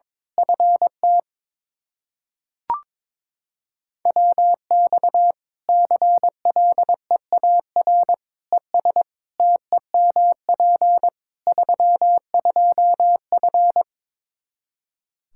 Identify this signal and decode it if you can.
Morse code, heard as a keyed tone, about 22 wpm, 700 Hz. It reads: FT WX CLEAR ES TEMP 32F
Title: Practice QSO - Exchange 2 - Info - 22wpm